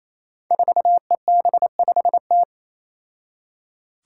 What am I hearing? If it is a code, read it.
Morse code, heard as a keyed tone, about 28 wpm, 700 Hz. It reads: 4EB5T
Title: Five Challenging Letter-Number Combinations 28wpm